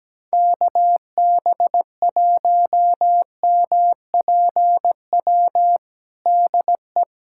Morse code, heard as a keyed tone, about 17 wpm, 700 Hz. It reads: KB1MPW DE
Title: USA Callsign DE Callsign - 6 Characters - 17wpm